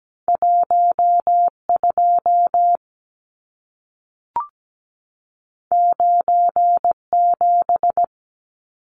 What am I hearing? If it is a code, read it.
Morse code, heard as a keyed tone, about 17 wpm, 700 Hz. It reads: 12 97
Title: Numbers - 2-Digit 17wpm